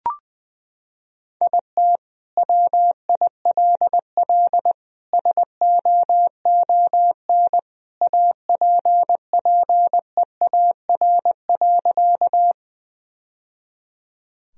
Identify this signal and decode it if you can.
Morse code, heard as a keyed tone, about 20 wpm, 700 Hz. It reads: IT WILL SOON APPEAR.
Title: Sentences from Top 1000 Words - Review Part 2 - 20wpm